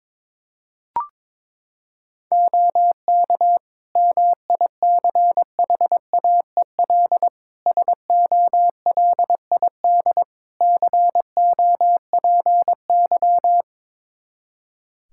Morse code, heard as a keyed tone, about 22 wpm, 700 Hz. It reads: OK MICHAEL SOLID COPY
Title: Practice QSO - Exchange 3 - More Info - 22wpm